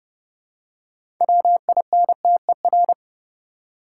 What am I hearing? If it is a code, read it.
Morse code, heard as a keyed tone, about 30 wpm, 700 Hz. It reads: WINTER
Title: Six Letter Words - 30wpm